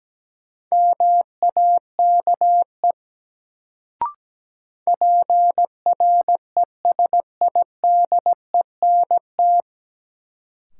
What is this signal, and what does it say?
Morse code, heard as a keyed tone, about 17 wpm, 700 Hz. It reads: MAKE PRESIDENT
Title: Sets of 2 Words - New Word Drill - 17wpm